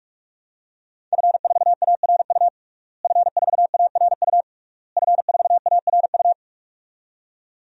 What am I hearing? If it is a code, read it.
Morse code, heard as a keyed tone, about 41 wpm, 700 Hz. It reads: F4ARU F4ARU F4ARU
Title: DXCC-100 Call Signs - 5 Characters - Repeated 3x - 45wpm